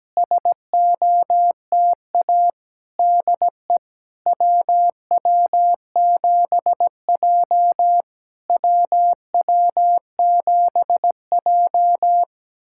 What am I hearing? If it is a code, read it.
Morse code, heard as a keyed tone, about 17 wpm, 700 Hz. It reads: SOTA DE WW7J WW7J
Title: SOTA QSO Practice 17wpm